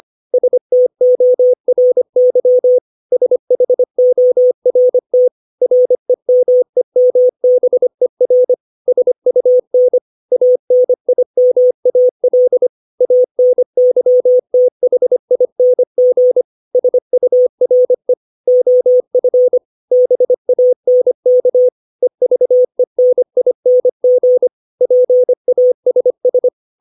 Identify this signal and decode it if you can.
Morse code, heard as a keyed tone, about 25 wpm, 500 Hz. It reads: STORY SHORT REMEMBER SUN ANIMAL ANYTHING SURE OF BANK EVENING PASS
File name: 632 words / 632 Words 25WPM